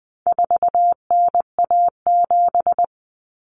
Morse code, heard as a keyed tone, about 20 wpm, 700 Hz. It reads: 4NA7
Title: Common Cut Numbers - 4-Digit 20wpm